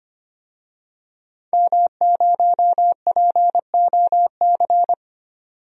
Morse code, heard as a keyed tone, about 25 wpm, 700 Hz. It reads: M0POC
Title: DXCC-100 Call Signs - 5 Characters - 25wpm